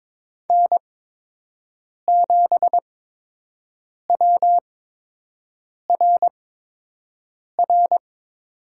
Morse code, heard as a keyed tone, about 22 wpm, spaced as for 6 wpm, 700 Hz. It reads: N7WRR
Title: USA Call Signs - 5 Characters - 8x Character Spacing - 22wpm